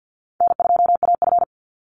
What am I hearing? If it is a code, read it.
Morse code, heard as a keyed tone, about 49 wpm, 700 Hz. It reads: N2AF
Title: NA SOTA Chaser Callsigns 50wpm